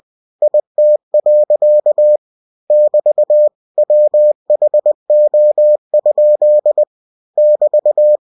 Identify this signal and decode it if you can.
Morse code, heard as a keyed tone, about 20 wpm, 600 Hz. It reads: IT. <BT> WHO? <BT>
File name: cw-books / winnie-the-pooh / winnie-20wpm-page-0032